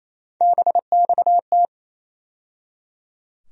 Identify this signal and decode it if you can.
Morse code, heard as a keyed tone, about 28 wpm, 700 Hz. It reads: BXT
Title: USA Call Sign Postfixes - 3 Letters 28wpm